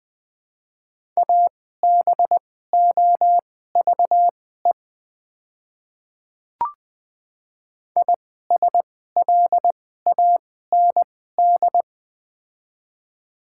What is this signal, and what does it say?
Morse code, heard as a keyed tone, about 20 wpm, spaced as for 15 wpm, 700 Hz. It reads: ABOVE ISLAND
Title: 2x Character Spacing - Top 500 Words - 20wpm